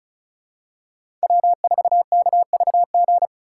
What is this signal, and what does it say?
Morse code, heard as a keyed tone, about 36 wpm, 700 Hz. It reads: W4KVG
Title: USA Call Signs - 4x Character Spacing - 35wpm